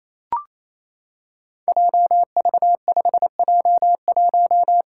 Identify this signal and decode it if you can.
Morse code, heard as a keyed tone, about 29 wpm, 700 Hz. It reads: JV5J1
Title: Five Challenging Letter-Number Combinations 28wpm